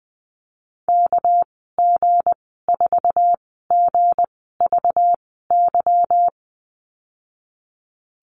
Morse code, heard as a keyed tone, about 20 wpm, spaced as for 15 wpm, 700 Hz. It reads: KG4GVY
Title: 207 Lesson 33g - Letter j - Call Signs 20wpm